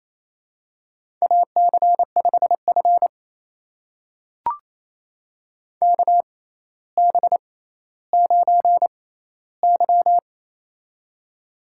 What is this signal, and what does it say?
Morse code, heard as a keyed tone, about 27 wpm, spaced as for 14 wpm, 700 Hz. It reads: AC5F KB9Y
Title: USA Call Signs - 4 Characters - 6x Character Spacing - 28wpm